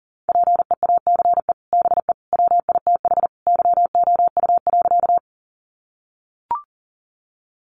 Morse code, heard as a keyed tone, about 40 wpm, 700 Hz. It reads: PEACE BE WITH YOU.
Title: Sentences from Top 1500 Words 40wpm